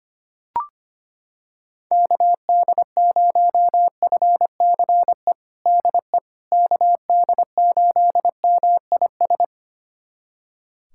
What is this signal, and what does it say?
Morse code, heard as a keyed tone, about 25 wpm, 700 Hz. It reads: KD0FCE DE KD8MIS
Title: USA Callsign DE Callsign 25wpm